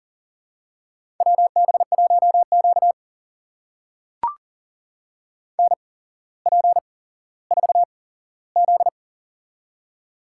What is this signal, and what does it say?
Morse code, heard as a keyed tone, about 40 wpm, spaced as for 17 wpm, 700 Hz. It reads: WB1Q NP4Z
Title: USA Call Signs - 4 Characters - 8x Character Spacing - 40wpm